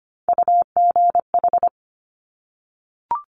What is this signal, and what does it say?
Morse code, heard as a keyed tone, about 25 wpm, 700 Hz. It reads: UGH
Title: Top Three Letter Combinations in Words 25wpm